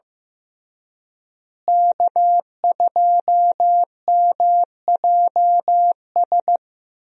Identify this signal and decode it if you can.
Morse code, heard as a keyed tone, about 15 wpm, 700 Hz. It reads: K2MJS
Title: USA Call Signs - 5 Characters - Repeated 3x - 15wpm